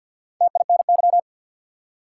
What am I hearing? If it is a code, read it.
Morse code, heard as a keyed tone, about 48 wpm, 700 Hz. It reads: TINY
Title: Four Letter Words - 50wpm